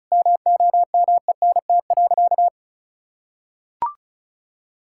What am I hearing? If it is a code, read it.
Morse code, heard as a keyed tone, about 34 wpm, 700 Hz. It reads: MOMENT.
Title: Sentences from Top 1000 Words - Review Part 6 - 35wpm